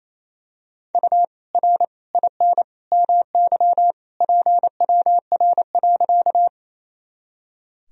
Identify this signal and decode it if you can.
Morse code, heard as a keyed tone, about 28 wpm, 700 Hz. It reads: U R IN MY PWR.
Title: Sentences from Top 500 Words and Abbreviations 28wpm